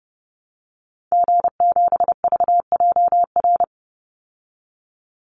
Morse code, heard as a keyed tone, about 30 wpm, 700 Hz. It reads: G7VJR
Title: Call Signs Repeated 3x - 30wpm